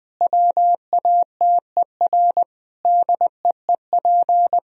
Morse code, heard as a keyed tone, about 20 wpm, 700 Hz. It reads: WATER DEEP
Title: Sentences from Top 1000 Words - Review Part 4 - 20wpm